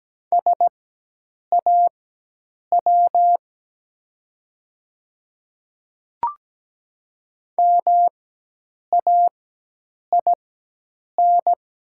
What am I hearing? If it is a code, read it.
Morse code, heard as a keyed tone, about 17 wpm, spaced as for 8 wpm, 700 Hz. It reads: SAW MAIN
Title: 4x Character Spacing - Top 300 Words - 17wpm